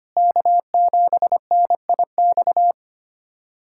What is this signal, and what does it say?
Morse code, heard as a keyed tone, about 25 wpm, 700 Hz. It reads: K7NIX
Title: USA Callsign DE Callsign 25wpm